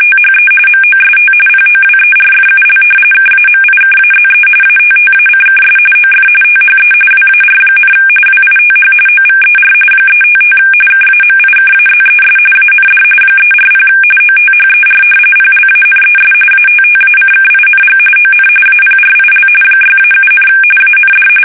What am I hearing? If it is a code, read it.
RTTY, about 45.45 baud. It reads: THE MOST POWERFUL KAMIKAZE ATTACKS OF THE WAR, WITH OVER 355 KAMIKAZES AND 340 OTHER PLANES ATTACKING ON APRIL 6-7, AND
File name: fsk850 / battleshipnj